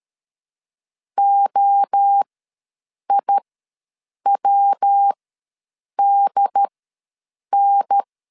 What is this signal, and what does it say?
Morse code, heard as a keyed tone, about 13 wpm, spaced as for 7 wpm, 800 Hz. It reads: OIWDN
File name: LEZIONI / 5 LBVC / 5 L ws 12wpm 2958